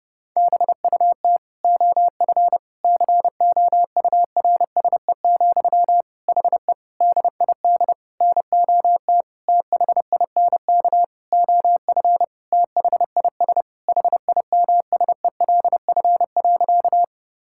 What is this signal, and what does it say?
Morse code, heard as a keyed tone, about 32 wpm, 700 Hz. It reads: BUT OF COURSE, HE DID NOT THINK OF THIS HIMSELF.
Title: Sentences from Top 1600 Words 30wpm